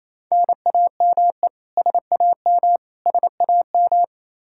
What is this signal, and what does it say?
Morse code, heard as a keyed tone, about 28 wpm, 700 Hz. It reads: NAME SAM SAM
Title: Practice QSO - Exchange 1 - Beginning - 28wpm